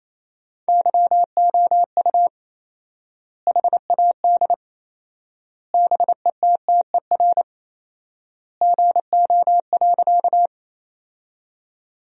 Morse code, heard as a keyed tone, about 28 wpm, 700 Hz. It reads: YOU HAD BETTER GO.
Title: 4x Word-Spacing - Sentences from Top 400 Words - 28wpm